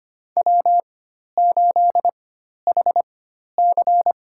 Morse code, heard as a keyed tone, about 25 wpm, spaced as for 12 wpm, 700 Hz. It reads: W8HC
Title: USA Call Signs - 4x Character Spacing - 25wpm